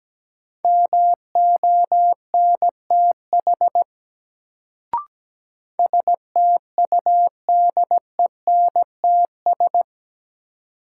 Morse code, heard as a keyed tone, about 17 wpm, 700 Hz. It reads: MONTH STUDENTS
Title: Sets of 5 Words Encore - New Word Drill - 17wpm